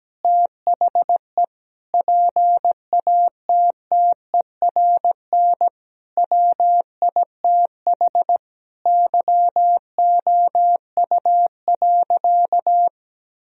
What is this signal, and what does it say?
Morse code, heard as a keyed tone, about 17 wpm, 700 Hz. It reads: THE PATTERN WITH YOU.
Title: Sentences from Top 1000 Words - Review Part 2 - 17wpm